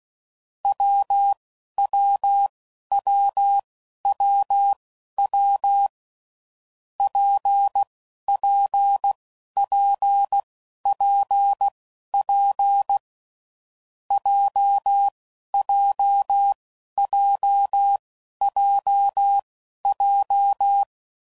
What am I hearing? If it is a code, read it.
Morse code, heard as a keyed tone, about 16 wpm, spaced as for 11 wpm, 800 Hz. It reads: WWWWW PPPPP JJJJJ
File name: Combinations / Difficult Combos / WPJ WPJ1